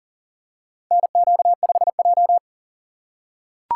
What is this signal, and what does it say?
Morse code, heard as a keyed tone, about 40 wpm, 700 Hz. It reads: NQ5J
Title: USA Call Signs - 4 Characters - 4x Character Spacing - 40wpm